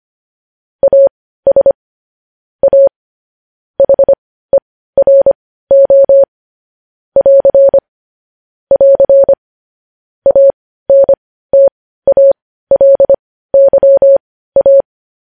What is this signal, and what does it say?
Morse code, heard as a keyed tone, about 25 wpm, spaced as for 15 wpm, 550 Hz. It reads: AS A HERO <AR> <AR> ANTALYA
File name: podcast / News headlines-1x-25x15